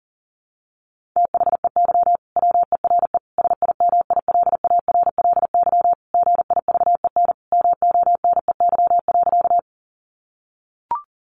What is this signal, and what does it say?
Morse code, heard as a keyed tone, about 40 wpm, 700 Hz. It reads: THEY WERE SIMILARLY GIVEN MONEY.
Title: Sentences from Top 1400 Words 40wpm